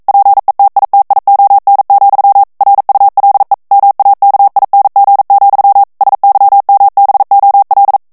Morse code, heard as a keyed tone, about 42 wpm, 800 Hz. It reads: PETITION, RULE MAKING, SYMBOL
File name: 40 mots minute